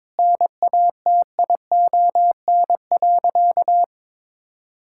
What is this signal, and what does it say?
Morse code, heard as a keyed tone, about 21 wpm, 700 Hz. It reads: NATION.
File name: practice / 22wpm / Sentences with 4-Word Phrases - 22wpm